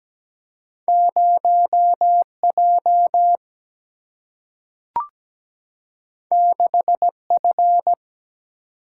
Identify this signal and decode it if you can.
Morse code, heard as a keyed tone, about 17 wpm, 700 Hz. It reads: 0J 6F
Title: Single-Letter Single-Number 17wpm